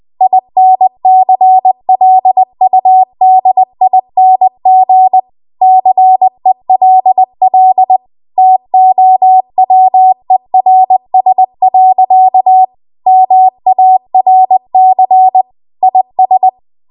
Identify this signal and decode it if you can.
Morse code, heard as a keyed tone, about 20 wpm, 750 Hz. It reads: INCLUDING CELL TOWERS. MARC IS
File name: Morse / 20WPM